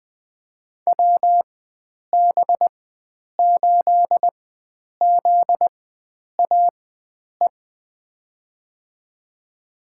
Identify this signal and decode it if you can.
Morse code, heard as a keyed tone, about 20 wpm, spaced as for 9 wpm, 700 Hz. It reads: WB8ZAE
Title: USA Call Signs - 6 Characters - 4x Character Spacing - 20wpm